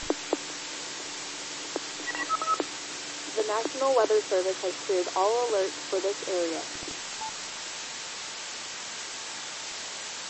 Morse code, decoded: none

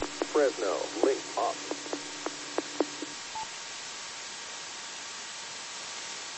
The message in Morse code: none

